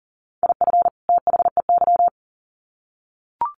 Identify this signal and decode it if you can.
Morse code, heard as a keyed tone, about 40 wpm, 700 Hz. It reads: IF THEY
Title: Sets of 2 Words 40wpm